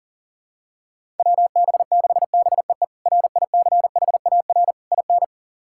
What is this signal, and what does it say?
Morse code, heard as a keyed tone, about 40 wpm, 690 Hz. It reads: WB6BEE RICHAR IN
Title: K1USN SST QSO Practice - 40wpm